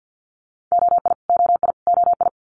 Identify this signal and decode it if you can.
Morse code, heard as a keyed tone, about 50 wpm, 700 Hz. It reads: OI OI OI
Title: Multi-letter Phonemes Repeated 3x 50wpm